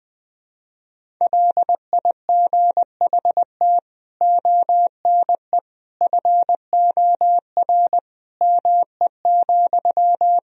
Morse code, heard as a keyed tone, about 20 wpm, 700 Hz. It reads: LIGHT ONE FOR ME,
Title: Sentences from Top 1000 Words - Review Part 1 - 20wpm